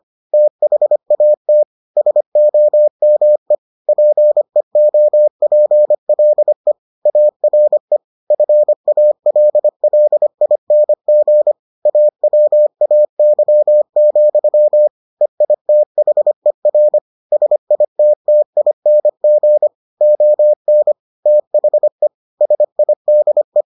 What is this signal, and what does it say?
Morse code, heard as a keyed tone, about 25 wpm, 600 Hz. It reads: THAT SOME PEOPLE ARE FALLING AWAY, EITHER SITTING ON THE SIDE